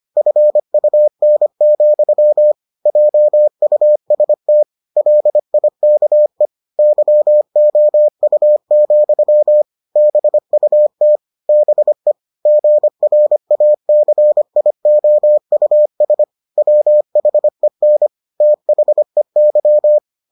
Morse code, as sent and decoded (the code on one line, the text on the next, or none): ..-. ..- -. --..-- / .--- ..- ... - / .-.. .. -.- . / -.-- --- ..- --..-- / -... ..- - / -... / --. .-. .- -.-. .. --- ..- ... / .-- .... -. / - .... -.--
FUN, JUST LIKE YOU, BUT BE GRACIOUS WHEN THEY